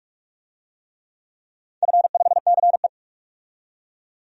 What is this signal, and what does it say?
Morse code, heard as a keyed tone, about 46 wpm, 700 Hz. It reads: F5CE
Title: DXCC-100 Call Signs - 4 Characters - 45wpm